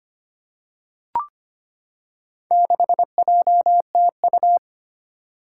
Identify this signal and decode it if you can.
Morse code, heard as a keyed tone, about 25 wpm, 700 Hz. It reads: 6JTU
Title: Four Challenging Letter-Number Combinations 25wpm